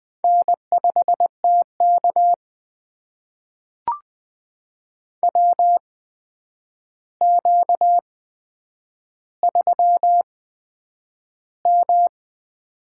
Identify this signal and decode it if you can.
Morse code, heard as a keyed tone, about 21 wpm, spaced as for 9 wpm, 700 Hz. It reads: N5TK WQ3M
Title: USA Call Signs - 4 Characters - 8x Character Spacing - 20wpm